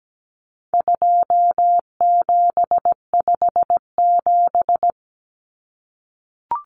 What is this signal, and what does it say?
Morse code, heard as a keyed tone, about 17 wpm, 700 Hz. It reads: 2757
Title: Common Cut Numbers - 4-Digit 17wpm